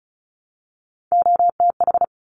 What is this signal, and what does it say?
Morse code, heard as a keyed tone, about 36 wpm, 700 Hz. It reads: OTH
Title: Top Three Letter Combinations in Words 35wpm